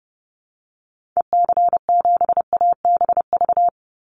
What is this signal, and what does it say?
Morse code, heard as a keyed tone, about 30 wpm, 700 Hz. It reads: EC7ABV
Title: Call Signs - 30wpm